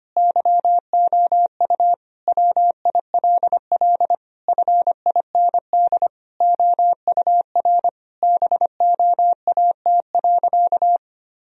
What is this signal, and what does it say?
Morse code, heard as a keyed tone, about 25 wpm, 700 Hz. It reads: YOU WILL FIND OUR BOAT.
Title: Sentences from Top 1000 Words - Review Part 4 - 25wpm